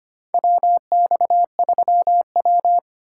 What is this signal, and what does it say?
Morse code, heard as a keyed tone, about 25 wpm, 700 Hz. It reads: WX3W
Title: USA Call Signs - 4 Characters - 3x Character Spacing - 25wpm